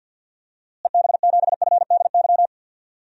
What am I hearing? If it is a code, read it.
Morse code, heard as a keyed tone, about 52 wpm, 700 Hz. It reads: EB7FDY
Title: DXCC-100 Call Signs - 6 Characters - Repeated 3x - 50wpm